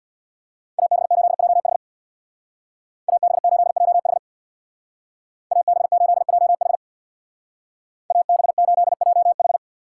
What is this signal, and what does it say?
Morse code, heard as a keyed tone, about 59 wpm, 700 Hz. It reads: AB8JH AB8JH AB8JH AB8JH
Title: Speed Racing - USA 2x2 Call Signs 50wpm